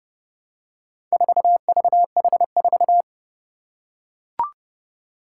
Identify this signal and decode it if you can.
Morse code, heard as a keyed tone, about 29 wpm, 700 Hz. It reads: 4VH4
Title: Four Challenging Letter-Number Combinations 30wpm